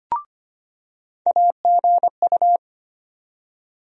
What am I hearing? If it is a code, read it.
Morse code, heard as a keyed tone, about 25 wpm, 700 Hz. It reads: AGU
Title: USA Call Sign Postfixes - 3 Letters 25wpm